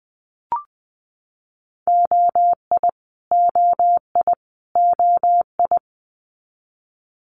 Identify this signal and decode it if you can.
Morse code, heard as a keyed tone, about 20 wpm, 700 Hz. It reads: OI OI OI
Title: Multi-letter Phonemes Repeated 3x 20wpm